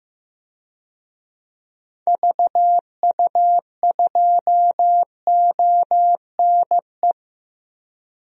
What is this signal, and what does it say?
Morse code, heard as a keyed tone, about 15 wpm, 700 Hz. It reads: VU2ONE
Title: DXCC-100 Call Signs - 6 Characters - 15wpm